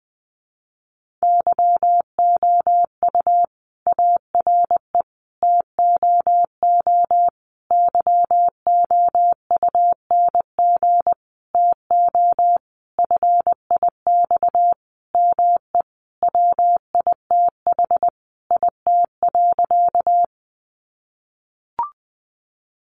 Morse code, heard as a keyed tone, about 20 wpm, 700 Hz. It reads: YOU ARE TOO YOUNG TO FIX ME WITH IT.
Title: Sentences from Top 1400 Words 20wpm